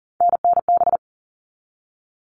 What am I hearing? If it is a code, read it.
Morse code, heard as a keyed tone, about 40 wpm, 700 Hz. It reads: NNB
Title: All Cut Numbers - 3-Digit 40wpm